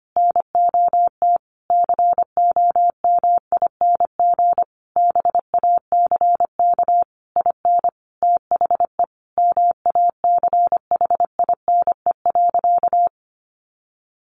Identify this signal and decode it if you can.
Morse code, heard as a keyed tone, about 25 wpm, 700 Hz. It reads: NOT COMING BACK IN THE MACHINE.
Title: A Sound Of Thunder 25wpm